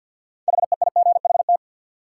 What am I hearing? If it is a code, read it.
Morse code, heard as a keyed tone, about 53 wpm, 700 Hz. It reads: HEIGHT
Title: Sets of 5 Words - New Word Drill - 50wpm